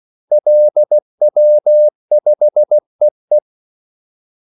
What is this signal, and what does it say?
Morse code, heard as a keyed tone, about 16 wpm, 600 Hz. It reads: LW5EE
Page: Kaldesignaler 16 wpm